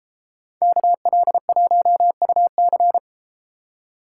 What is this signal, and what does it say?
Morse code, heard as a keyed tone, about 31 wpm, 700 Hz. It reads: KL1UC